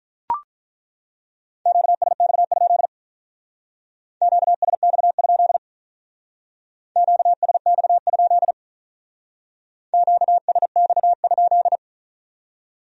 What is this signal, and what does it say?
Morse code, heard as a keyed tone, about 42 wpm, 700 Hz. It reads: QSX? QSX? QSX? QSX?
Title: Speed-Racing Q-Codes - 35wpm